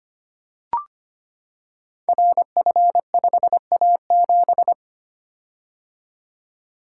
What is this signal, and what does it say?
Morse code, heard as a keyed tone, about 25 wpm, 700 Hz. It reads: RF5A7
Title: Code Groups - 5 Letter:Numbers - 25wpm